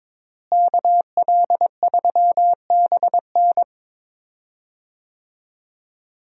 Morse code, heard as a keyed tone, about 22 wpm, 700 Hz. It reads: KL3BN
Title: USA Call Signs - 5 Characters - Repeated 3x - 22wpm